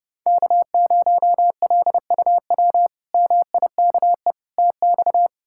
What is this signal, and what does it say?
Morse code, heard as a keyed tone, about 32 wpm, 700 Hz. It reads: K0LUW MIKE TX
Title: K1USN SST QSO Practice - 30wpm